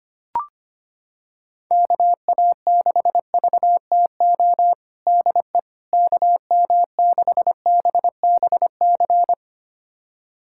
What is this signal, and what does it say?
Morse code, heard as a keyed tone, about 25 wpm, 700 Hz. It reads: KA6VTO DE KM6BBC
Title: USA Callsign DE Callsign 25wpm